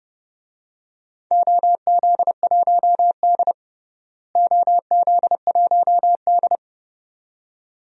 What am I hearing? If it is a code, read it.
Morse code, heard as a keyed tone, about 31 wpm, 700 Hz. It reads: OZ1D OZ1D
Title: DXCC-100 Call Signs - 4 Characters - Repeated 2x - 30wpm